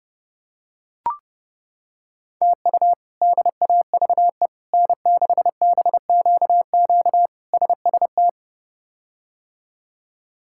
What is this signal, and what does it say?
Morse code, heard as a keyed tone, about 30 wpm, 700 Hz. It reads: TU DAVE N6BQQ SST
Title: K1USN SST QSO Practice - 30wpm